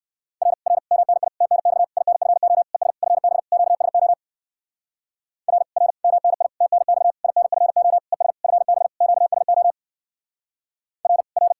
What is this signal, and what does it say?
Morse code, heard as a keyed tone, about 57 wpm, 700 Hz. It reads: R R MNI TNX INFO ES FB QSO R R MNI TNX INFO ES FB QSO R R